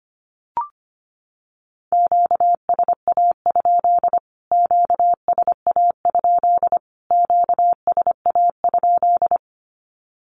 Morse code, heard as a keyed tone, about 25 wpm, 700 Hz. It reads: QSA? QSA? QSA?
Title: Q-Codes Repeated 3x 25wpm